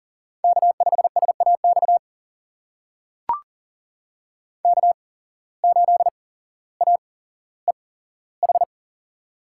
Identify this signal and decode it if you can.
Morse code, heard as a keyed tone, about 39 wpm, spaced as for 17 wpm, 700 Hz. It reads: K5SAX K8AEH
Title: USA Call Signs - 5 Characters - 8x Character Spacing - 40wpm